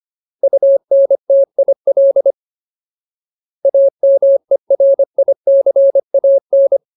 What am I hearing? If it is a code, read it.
Morse code, heard as a keyed tone, about 25 wpm, 550 Hz. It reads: UNTIL AMERICAN
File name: Ultimate Word List / 25WPM Ultimate